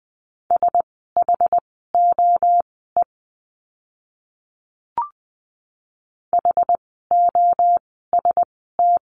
Morse code, heard as a keyed tone, about 20 wpm, spaced as for 14 wpm, 700 Hz. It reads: SHOE HOST
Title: 068 Lesson 12f - Letter d - Words 20wpm